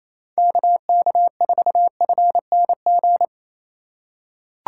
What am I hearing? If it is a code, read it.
Morse code, heard as a keyed tone, about 28 wpm, 700 Hz. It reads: KK4FNG
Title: USA Call Signs - 6 Characters - 4x Character Spacing - 28wpm